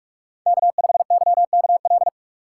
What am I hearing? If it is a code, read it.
Morse code, heard as a keyed tone, about 42 wpm, 700 Hz. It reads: K5YKL
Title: USA Call Signs - 5 Characters - 3x Character Spacing - 45wpm